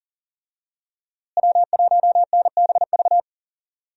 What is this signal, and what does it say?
Morse code, heard as a keyed tone, about 40 wpm, 700 Hz. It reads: W1NBV
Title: USA Call Signs - 5 Characters - Repeated 2x - 40wpm